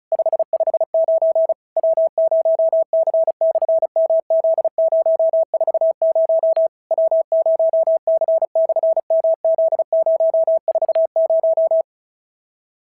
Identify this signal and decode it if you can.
Morse code, heard as a keyed tone, about 36 wpm, 690 Hz. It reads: 559 W0C/MZ040 W0C/MZ040
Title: SOTA QSO Practice 35wpm